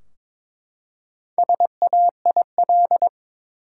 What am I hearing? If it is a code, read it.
Morse code, heard as a keyed tone, about 22 wpm, 700 Hz. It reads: SAIL
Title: Four Letter Words - 22wpm